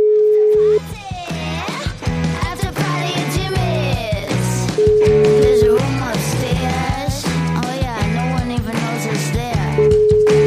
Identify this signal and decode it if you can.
Ringback tone (Europe, 425 Hz), with noise.